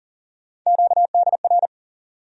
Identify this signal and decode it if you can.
Morse code, heard as a keyed tone, about 39 wpm, 700 Hz. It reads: QDR
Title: USA Call Sign Postfixes - 3 Letters 40wpm